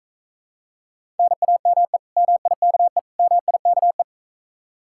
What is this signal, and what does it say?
Morse code, heard as a keyed tone, about 45 wpm, 700 Hz. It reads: NAME MIKE MIKE